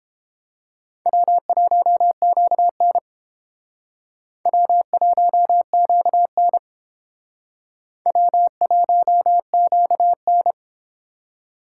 Morse code, heard as a keyed tone, about 30 wpm, 700 Hz. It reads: W1QN W1QN W1QN